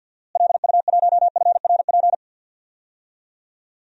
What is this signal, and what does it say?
Morse code, heard as a keyed tone, about 50 wpm, 700 Hz. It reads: LU1FRP
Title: DXCC-100 Call Signs - 6 Characters - Repeated 2x - 50wpm